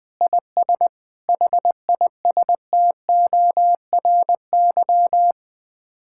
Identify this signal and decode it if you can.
Morse code, heard as a keyed tone, about 20 wpm, 700 Hz. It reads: IS HISTORY
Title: Sets of 8 Words - 20wpm